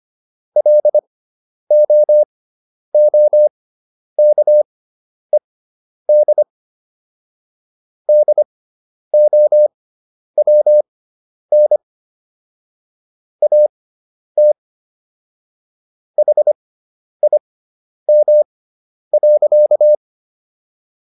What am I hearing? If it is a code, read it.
Morse code, heard as a keyed tone, about 25 wpm, spaced as for 10 wpm, 600 Hz. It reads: LOOKED DOWN AT HIM.